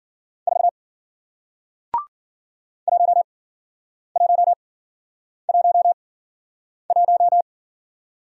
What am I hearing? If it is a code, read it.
Morse code, heard as a keyed tone, about 55 wpm, spaced as for 9 wpm, 700 Hz. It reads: 4 1111